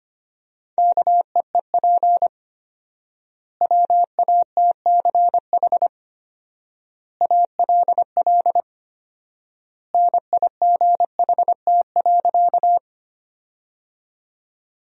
Morse code, heard as a keyed tone, about 25 wpm, 700 Hz. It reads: KEEP WATCH ALL NIGHT.
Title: 4x Word-Spacing - Sentences from Top 300 Words - 25wpm